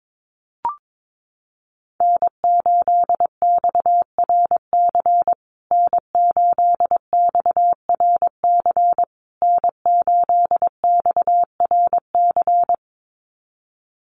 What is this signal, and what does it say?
Morse code, heard as a keyed tone, about 22 wpm, 700 Hz. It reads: N8XRC N8XRC N8XRC
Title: USA Call Signs Repeated 3x 22wpm